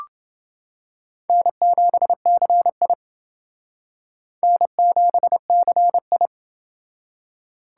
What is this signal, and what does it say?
Morse code, heard as a keyed tone, about 29 wpm, 700 Hz. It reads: N7CI N7CI